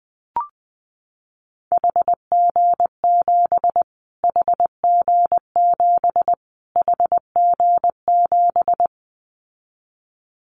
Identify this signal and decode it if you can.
Morse code, heard as a keyed tone, about 20 wpm, 700 Hz. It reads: HG7 HG7 HG7
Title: Callsign Prefixes for Top 25 DXCC - Repeated 3x - 20wpm